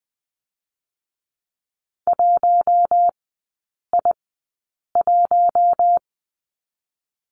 Morse code, heard as a keyed tone, about 20 wpm, spaced as for 8 wpm, 700 Hz. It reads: 1I1
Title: 081 Lesson 14e - Number 2 - Three Characters 20wpm